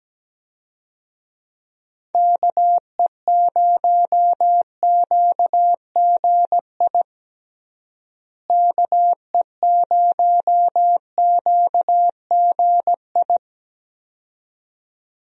Morse code, heard as a keyed tone, about 17 wpm, 700 Hz. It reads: KE0QGI KE0QGI
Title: USA Call Signs - 6 Characters - Repeated 2x - 17wpm